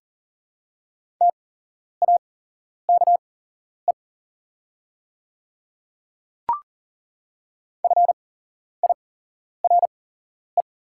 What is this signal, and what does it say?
Morse code, heard as a keyed tone, about 40 wpm, spaced as for 11 wpm, 700 Hz. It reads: TAKE FIRE
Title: 8x Character Spacing - Top 500 Words - 40wpm